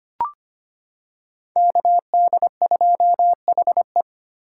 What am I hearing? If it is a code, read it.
Morse code, heard as a keyed tone, about 25 wpm, 700 Hz. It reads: KD2HE
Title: K1USN SST QSO Practice - 25wpm